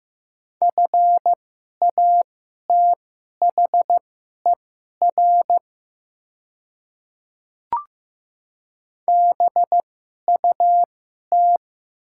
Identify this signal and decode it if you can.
Morse code, heard as a keyed tone, about 15 wpm, spaced as for 11 wpm, 700 Hz. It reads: FATHER BUT
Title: 2x Character Spacing - Top 200 Words - 15wpm